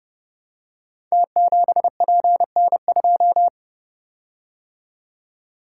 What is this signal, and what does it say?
Morse code, heard as a keyed tone, about 30 wpm, 700 Hz. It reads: T7PN2
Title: Code Groups - 5 Letter:Numbers - 30wpm